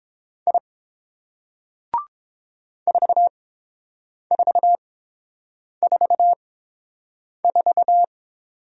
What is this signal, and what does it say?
Morse code, heard as a keyed tone, about 29 wpm, spaced as for 7 wpm, 700 Hz. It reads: I 4444